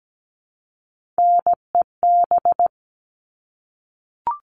Morse code, heard as a keyed tone, about 17 wpm, 700 Hz. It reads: NEB
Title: All Cut Numbers - 3-Digit 17wpm